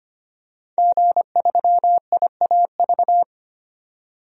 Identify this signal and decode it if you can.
Morse code, heard as a keyed tone, about 25 wpm, 700 Hz. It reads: G3IAV
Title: DXCC-100 Call Signs - 5 Characters - 25wpm